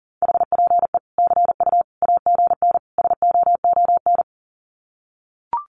Morse code, heard as a keyed tone, about 40 wpm, 700 Hz. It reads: HPE CU AGN SOON
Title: Common QSO Phrases 40wpm